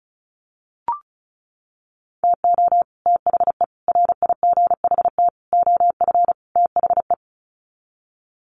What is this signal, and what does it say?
Morse code, heard as a keyed tone, about 34 wpm, 700 Hz. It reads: TO THE RIGHT OF THE
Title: Sets of 5 Words Encore 35wpm